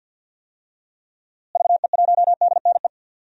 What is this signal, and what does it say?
Morse code, heard as a keyed tone, about 52 wpm, 700 Hz. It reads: VE1DNE
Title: DXCC-100 Call Signs - 6 Characters - Repeated 2x - 50wpm